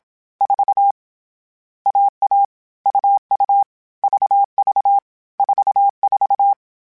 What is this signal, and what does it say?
Morse code, heard as a keyed tone, about 27 wpm, 800 Hz. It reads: 4 AA UU VV 44
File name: CW Code Groups-mp3s / Sets mp3s / AUV AUV4 sp @25